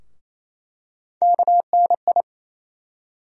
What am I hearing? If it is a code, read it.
Morse code, heard as a keyed tone, about 28 wpm, 700 Hz. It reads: KNI
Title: USA Call Sign Postfixes - 3 Letters 28wpm